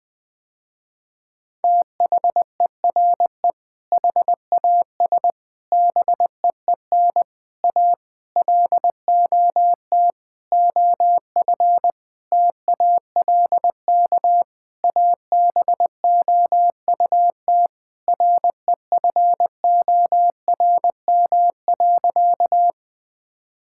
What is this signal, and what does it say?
Morse code, heard as a keyed tone, about 20 wpm, 700 Hz. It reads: THERE HAS BEEN A LOT OF TALK ABOUT REFORM.
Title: Sentences with 7-Word Phrases - Spoken Beginning - 20wpm